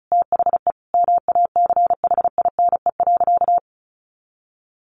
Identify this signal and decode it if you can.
Morse code, heard as a keyed tone, about 34 wpm, 700 Hz. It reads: THE MACHINE.
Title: A Sound Of Thunder 35wpm